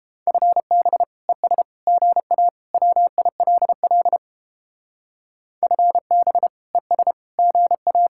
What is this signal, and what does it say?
Morse code, heard as a keyed tone, about 29 wpm, 700 Hz. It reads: FB ES GA WILL FB ES GA